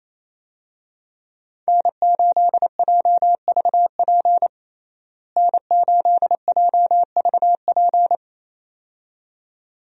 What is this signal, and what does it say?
Morse code, heard as a keyed tone, about 28 wpm, 700 Hz. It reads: N8JVP N8JVP
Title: USA Call Signs - 5 Characters - Repeated 2x - 28wpm